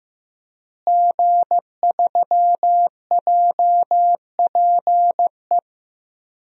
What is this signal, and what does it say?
Morse code, heard as a keyed tone, about 15 wpm, 700 Hz. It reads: G3JPE
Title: DXCC-100 Call Signs - 5 Characters - Repeated 3x - 15wpm